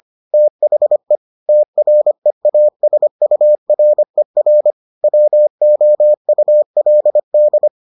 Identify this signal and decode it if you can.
Morse code, heard as a keyed tone, about 25 wpm, 600 Hz. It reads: THE TREASURER WOULD